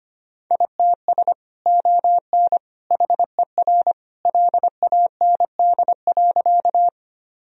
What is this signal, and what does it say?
Morse code, heard as a keyed tone, about 25 wpm, 700 Hz. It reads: ITS ON HER LAND.
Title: Sentences from Top 1000 Words - Review Part 1 - 25wpm